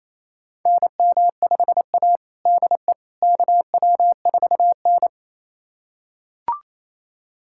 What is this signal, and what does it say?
Morse code, heard as a keyed tone, about 27 wpm, 700 Hz. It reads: NM5A DE KW4N
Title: USA Callsign DE Callsign - 4 Characters - 28wpm